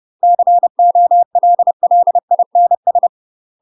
Morse code, heard as a keyed tone, about 30 wpm, 700 Hz. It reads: COLLINS
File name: UK MASTER QSOs TRAINING / 30WPM / UK MASTER QSOs 0019